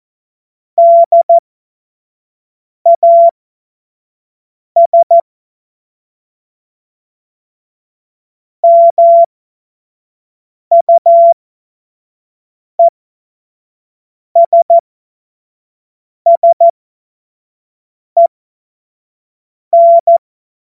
Morse code, heard as a keyed tone, about 14 wpm, spaced as for 5 wpm, 680 Hz. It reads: DAS MUESSEN
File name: Morsekurs / Witze / Witz 18. April 2023 25 F70